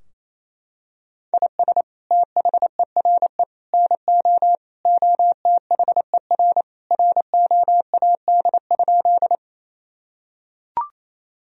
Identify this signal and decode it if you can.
Morse code, heard as a keyed tone, about 28 wpm, 700 Hz. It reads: IS THERE NO OTHER ROAD?
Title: Sentences from Top 1000 Words - Review Part 2 - 28wpm